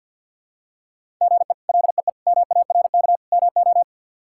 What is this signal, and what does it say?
Morse code, heard as a keyed tone, about 46 wpm, 700 Hz. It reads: GE LEE MARK MO
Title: K1USN SST QSO Practice - 50wpm